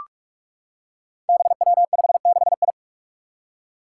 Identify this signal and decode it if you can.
Morse code, heard as a keyed tone, about 46 wpm, 700 Hz. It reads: BW56I
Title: Five Challenging Letter-Number Combinations 45wpm